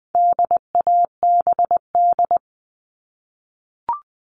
Morse code, heard as a keyed tone, about 20 wpm, 700 Hz. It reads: DABD
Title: All Cut Numbers - 4-Digit 20wpm